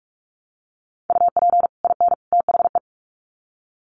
Morse code, heard as a keyed tone, about 44 wpm, 700 Hz. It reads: UP IN THE
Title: Sets of 3 Words 45wpm